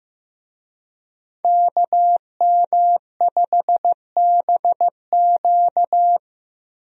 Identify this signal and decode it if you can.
Morse code, heard as a keyed tone, about 15 wpm, 700 Hz. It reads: KM5BQ
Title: USA Call Signs - 5 Characters - 15wpm